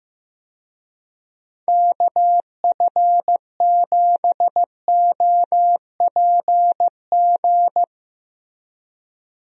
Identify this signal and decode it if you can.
Morse code, heard as a keyed tone, about 16 wpm, 700 Hz. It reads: KF7OPG
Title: USA Call Signs - 6 Characters - Repeated 3x - 15wpm